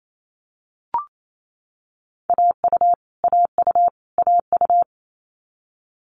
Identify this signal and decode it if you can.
Morse code, heard as a keyed tone, about 28 wpm, 700 Hz. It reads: AU AU AU
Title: Multi-letter Phonemes Repeated 3x 28wpm